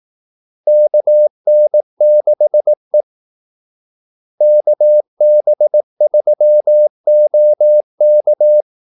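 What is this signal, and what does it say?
Morse code, heard as a keyed tone, about 18 wpm, 600 Hz. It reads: KN6E KB3OK
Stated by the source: Morse call-sign practice at 18 wpm.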